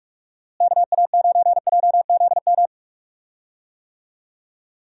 Morse code, heard as a keyed tone, about 43 wpm, 700 Hz. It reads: KA9JZM
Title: USA Call Signs - 6 Characters - Repeated 2x - 45wpm